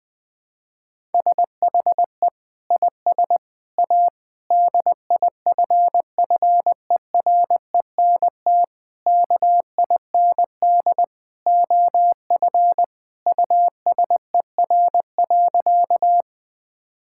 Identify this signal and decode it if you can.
Morse code, heard as a keyed tone, about 20 wpm, 700 Hz. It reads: SHE IS A DIFFERENT KIND OF USER.
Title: Sentences from Top 1000 Words - Review Part 5 - 20wpm